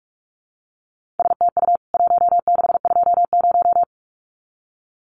Morse code, heard as a keyed tone, about 44 wpm, 700 Hz. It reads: STU 1620
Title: CWT Exchange 45wpm